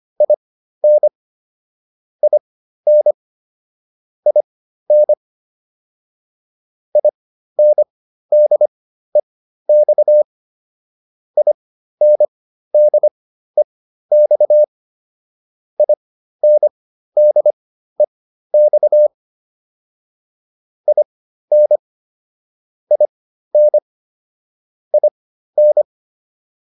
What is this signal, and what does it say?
Morse code, heard as a keyed tone, about 25 wpm, spaced as for 13 wpm, 600 Hz. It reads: IN IN IN INDEX INDEX INDEX IN IN IN